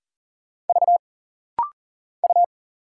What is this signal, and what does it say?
Morse code, heard as a keyed tone, about 40 wpm, 700 Hz. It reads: V U
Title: Single Letter-Number Alt - Rapid-Fire 40wpm